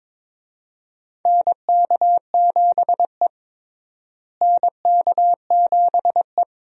Morse code, heard as a keyed tone, about 21 wpm, 700 Hz. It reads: NK7E NK7E
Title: USA Call Signs - 4 Characters - Repeated 2x - 22wpm